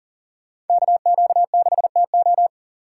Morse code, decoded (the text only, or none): KQ6TO